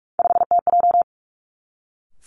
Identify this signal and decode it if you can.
Morse code, heard as a keyed tone, about 46 wpm, 700 Hz. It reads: 5TJ
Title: Three Challenging Letter-Number Combinations 45wpm